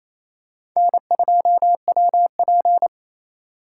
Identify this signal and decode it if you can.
Morse code, heard as a keyed tone, about 28 wpm, 700 Hz. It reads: N2WP
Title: USA Call Signs - 4x Character Spacing - 28wpm